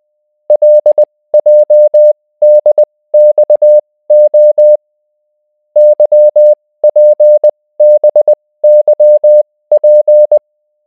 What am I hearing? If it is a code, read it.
Morse code, heard as a keyed tone, about 20 wpm, spaced as for 16 wpm, 600 Hz. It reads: LJDXO YPBYP